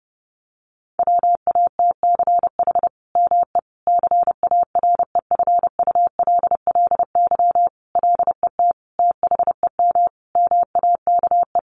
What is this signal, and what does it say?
Morse code, heard as a keyed tone, about 30 wpm, 700 Hz. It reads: WATCH ME CAREFULLY LET THEM MAKE
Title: Sentences from Top 1300 Words 30wpm